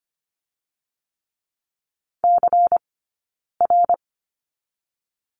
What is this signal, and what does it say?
Morse code, heard as a keyed tone, about 25 wpm, spaced as for 9 wpm, 700 Hz. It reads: CR
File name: course / 25:10wpm / 099 Lesson 17d - Letter u - Two Characters 25wpm